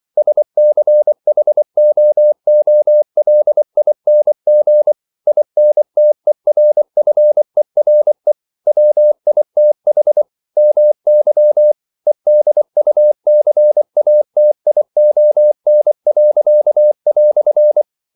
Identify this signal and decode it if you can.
Morse code, heard as a keyed tone, about 24 wpm, 600 Hz. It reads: SCHOOLING INTERFERE WITH MY EDUCATION."
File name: CW / Practice / EnglishQuotations0424